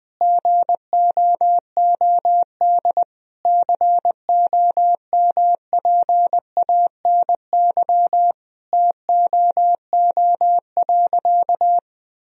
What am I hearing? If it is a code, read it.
Morse code, heard as a keyed tone, about 19 wpm, 700 Hz. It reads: GOOD COMPANY TOO.
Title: Sentences from Top 1000 Words - Review Part 3 - 20wpm